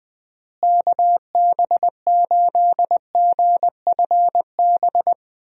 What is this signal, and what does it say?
Morse code, heard as a keyed tone, about 20 wpm, 700 Hz. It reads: KB8GFB
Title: USA Call Signs - 6 Characters - Repeated 2x - 20wpm